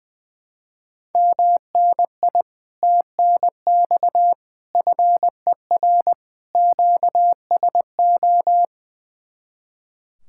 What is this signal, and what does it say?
Morse code, heard as a keyed tone, about 20 wpm, 700 Hz. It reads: MNI TNX FER QSO
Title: Speed-Racing Common QSO Phrases - 20wpm